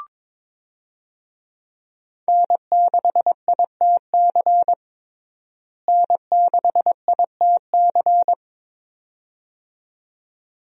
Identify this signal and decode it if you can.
Morse code, heard as a keyed tone, about 22 wpm, 700 Hz. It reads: N6ITC N6ITC
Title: USA Call Signs - 5 Characters - Repeated 2x - 22wpm